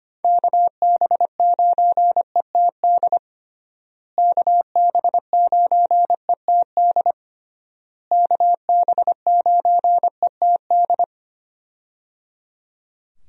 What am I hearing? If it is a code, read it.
Morse code, heard as a keyed tone, about 25 wpm, 700 Hz. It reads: KB9ETD KB9ETD KB9ETD
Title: USA Call Signs - 6 Characters - Repeated 3x - 25wpm